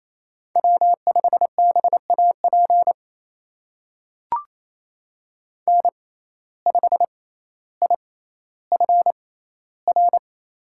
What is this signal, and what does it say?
Morse code, heard as a keyed tone, about 28 wpm, spaced as for 14 wpm, 700 Hz. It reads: W5BAP N5IFR
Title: USA Call Signs - 5 Characters - 6x Character Spacing - 28wpm